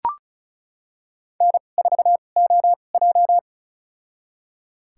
Morse code, heard as a keyed tone, about 34 wpm, spaced as for 25 wpm, 700 Hz. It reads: N4OJ
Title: USA Call Signs - 4 Characters - 2x Character Spacing - 35wpm